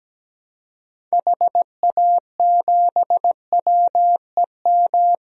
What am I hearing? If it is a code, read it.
Morse code, heard as a keyed tone, about 17 wpm, 700 Hz. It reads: HA7WEM
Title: DXCC-100 Call Signs - 6 Characters - 17wpm